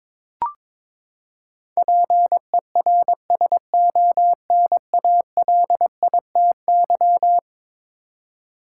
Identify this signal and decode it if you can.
Morse code, heard as a keyed tone, about 22 wpm, 700 Hz. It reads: PERSONALITY
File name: practice / 22wpm / Eleven Letter Words - 22wpm